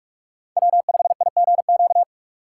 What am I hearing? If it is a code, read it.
Morse code, heard as a keyed tone, about 46 wpm, 700 Hz. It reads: W5IGQ
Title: USA Call Signs - 3x Character Spacing - 45wpm